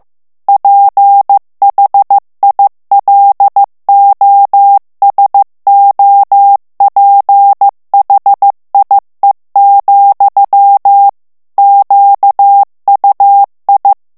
Morse code, heard as a keyed tone, about 15 wpm, 800 Hz. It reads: PHILOSOPHIE, QUI
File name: morse mp3 / FRANCAIS TEXTE CLAIR 2 VITESSE 14 MOTS MINUTES